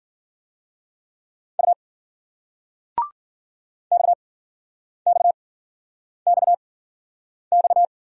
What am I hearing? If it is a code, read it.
Morse code, heard as a keyed tone, about 54 wpm, spaced as for 9 wpm, 700 Hz. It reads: U XXXX